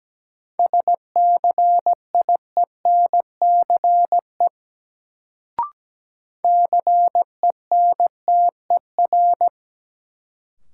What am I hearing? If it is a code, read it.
Morse code, heard as a keyed tone, about 17 wpm, 700 Hz. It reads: SCIENCE CENTER
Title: Sentences from Top 1000 Words - Review Part 2 - New Word Drill - 17wpm